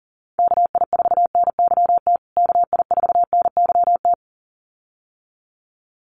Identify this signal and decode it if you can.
Morse code, heard as a keyed tone, about 40 wpm, 700 Hz. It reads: KI4NYT KI4NYT
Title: USA Call Signs Repeated 2x 40wpm